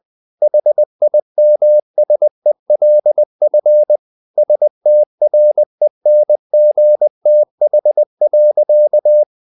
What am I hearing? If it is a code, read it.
Morse code, heard as a keyed tone, about 20 wpm, 600 Hz. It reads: HIMSELF STRENGTH.
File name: the old man and the sea / the old man and the sea part 1 20wpm